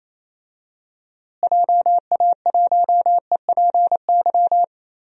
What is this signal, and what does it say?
Morse code, heard as a keyed tone, about 28 wpm, 700 Hz. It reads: JA1EPY
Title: DXCC-100 Call Signs - 6 Characters - Repeated 2x - 28wpm